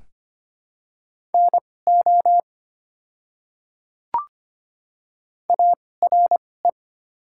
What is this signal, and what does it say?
Morse code, heard as a keyed tone, about 25 wpm, spaced as for 18 wpm, 700 Hz. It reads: NO ARE
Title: 2x Character Spacing - Top 300 Words - 25wpm